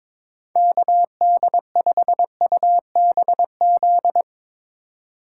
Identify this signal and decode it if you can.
Morse code, heard as a keyed tone, about 22 wpm, 700 Hz. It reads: KD5UBZ
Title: USA Call Signs - 6 Characters - 4x Character Spacing - 22wpm